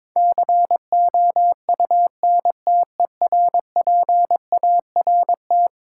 Morse code, heard as a keyed tone, about 22 wpm, 700 Hz. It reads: COUNTERPART
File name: practice / 22wpm / Eleven Letter Words - 22wpm